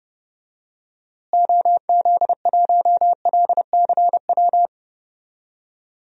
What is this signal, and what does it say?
Morse code, heard as a keyed tone, about 31 wpm, 700 Hz. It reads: OZ1LCW
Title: DXCC-100 Call Signs - 6 Characters - Repeated 2x - 30wpm